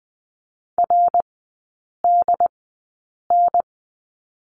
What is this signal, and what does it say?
Morse code, heard as a keyed tone, about 20 wpm, spaced as for 8 wpm, 700 Hz. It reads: RDN
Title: 067 Lesson 12e - Letter d - Three Characters 20wpm